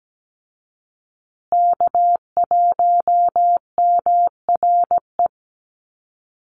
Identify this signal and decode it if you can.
Morse code, heard as a keyed tone, about 17 wpm, 700 Hz. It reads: K1MRE
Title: USA Call Signs Repeated 2x 17wpm